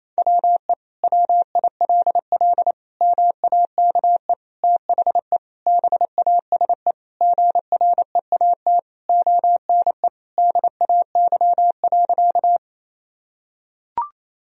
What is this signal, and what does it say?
Morse code, heard as a keyed tone, about 27 wpm, 700 Hz. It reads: WE WILL MAKE THE BASE GREAT ONE DAY.
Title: Sentences from Top 1000 Words - Review Part 1 - 28wpm